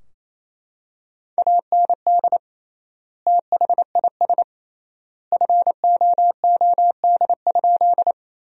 Morse code, heard as a keyed tone, about 28 wpm, 700 Hz. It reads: AND THIS FOOD?
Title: 3x Word-Spacing - Sentences from Top 300 Words - 28wpm